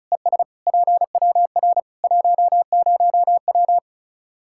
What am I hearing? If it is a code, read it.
Morse code, heard as a keyed tone, about 36 wpm, 700 Hz. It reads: ES PWR 10W
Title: Practice QSO - Exchange 2 - Info - 35wpm